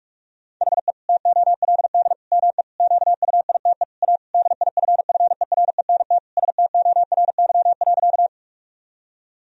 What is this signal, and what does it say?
Morse code, heard as a keyed tone, about 46 wpm, 700 Hz. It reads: HE TOLD ME QUITE A DIFFERENT STORY.
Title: Sentences from Top 1000 Words - Review Part 3 - 45wpm